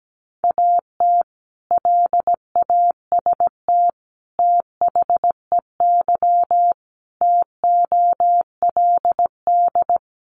Morse code, heard as a keyed tone, about 17 wpm, 700 Hz. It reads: AT LAST THEY TOLD
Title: Sentences from Top 1200 Words 17wpm